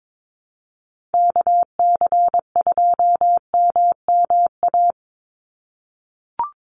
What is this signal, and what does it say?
Morse code, heard as a keyed tone, about 22 wpm, 700 Hz. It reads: KC2MMA
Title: USA Call Signs 22wpm